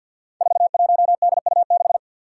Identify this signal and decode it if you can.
Morse code, heard as a keyed tone, about 51 wpm, 700 Hz. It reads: 41DU6
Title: Five Challenging Letter-Number Combinations 50wpm